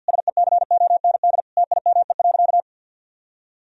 Morse code, heard as a keyed tone, about 51 wpm, 700 Hz. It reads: SECOND TIME.
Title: Sentences from Top 1000 Words - Review Part 1 - 50wpm